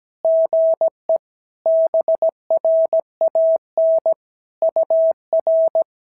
Morse code, heard as a keyed tone, about 17 wpm, 650 Hz. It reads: GE BRAN UR
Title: SOTA QSO Practice 17wpm